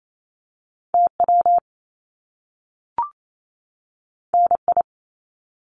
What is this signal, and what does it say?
Morse code, heard as a keyed tone, about 28 wpm, 700 Hz. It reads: TW NI
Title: Top Two Letter Combinations in Words 28wpm